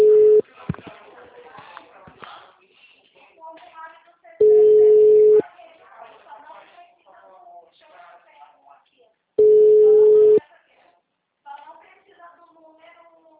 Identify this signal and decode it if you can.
Ringback tone (Europe, 425 Hz), with speech.
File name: out- -2012- - - .3947